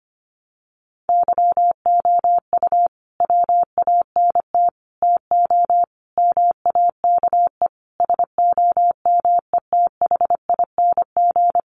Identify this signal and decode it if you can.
Morse code, heard as a keyed tone, about 25 wpm, 700 Hz. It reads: YOU WANT TO MAKE SOMETHING
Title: Sentences from Top 600 Words 25wpm